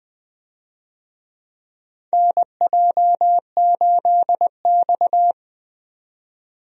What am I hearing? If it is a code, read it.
Morse code, heard as a keyed tone, about 20 wpm, 700 Hz. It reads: NJ8X
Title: USA Call Signs - 4 Characters - 20wpm